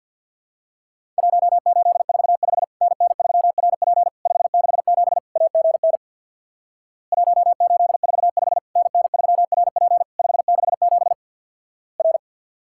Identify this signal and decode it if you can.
Morse code, heard as a keyed tone, about 50 wpm, 700 Hz. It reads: 1845 NN3RP 567 AGN 1845 NN3RP 567 R
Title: Worked All Europe DX Contest - QTC Exchange with Repeats - 50wpm